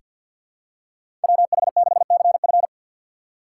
Morse code, heard as a keyed tone, about 47 wpm, 700 Hz. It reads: WH6CF
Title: USA Call Signs - 5 Characters - 6x Character Spacing - 50wpm